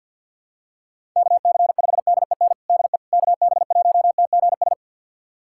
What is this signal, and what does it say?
Morse code, heard as a keyed tone, about 48 wpm, 690 Hz. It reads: KC5DEN DE KB1TGS
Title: USA Callsign DE Callsign - 6 Characters - 50wpm